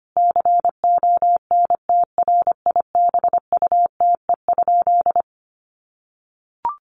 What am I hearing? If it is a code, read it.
Morse code, heard as a keyed tone, about 25 wpm, 700 Hz. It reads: CONTRIBUTE?
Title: Sentences from Top 1400 Words 25wpm